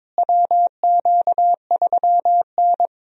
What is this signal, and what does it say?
Morse code, heard as a keyed tone, about 22 wpm, 700 Hz. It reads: WQ3N
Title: USA Call Signs - 4 Characters - Repeated 3x - 22wpm